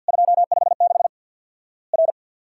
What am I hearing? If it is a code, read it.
Morse code, heard as a keyed tone, about 50 wpm, 700 Hz. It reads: 256 R
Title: Worked All Europe DX Contest - QTC Exchange - 50wpm